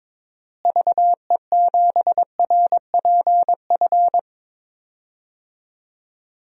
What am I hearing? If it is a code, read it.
Morse code, heard as a keyed tone, about 22 wpm, 700 Hz. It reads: VE7RPF
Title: DXCC-100 Call Signs - 6 Characters - Repeated 2x - 22wpm